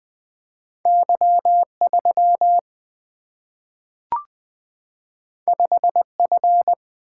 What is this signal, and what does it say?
Morse code, heard as a keyed tone, about 20 wpm, 700 Hz. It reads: Y 3 5 F
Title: Single-Letter Single-Number 20wpm